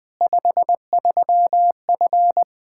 Morse code, heard as a keyed tone, about 20 wpm, 700 Hz. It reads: 53F
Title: Practice QSO - Exchange 2 - Info - 20wpm